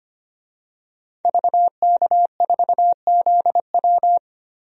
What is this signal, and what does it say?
Morse code, heard as a keyed tone, about 25 wpm, 700 Hz. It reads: VK4ZW
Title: DXCC-100 Call Signs - 5 Characters - 25wpm